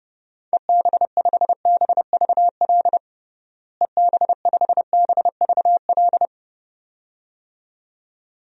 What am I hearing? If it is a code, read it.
Morse code, heard as a keyed tone, about 30 wpm, 700 Hz. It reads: EB5BVL EB5BVL
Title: DXCC-100 Call Signs - 6 Characters - Repeated 2x - 30wpm